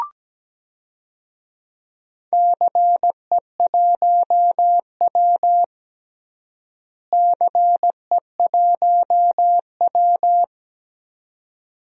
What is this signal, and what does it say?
Morse code, heard as a keyed tone, about 17 wpm, 700 Hz. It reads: CE1W CE1W
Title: DXCC-100 Call Signs - 4 Characters - Repeated 2x - 17wpm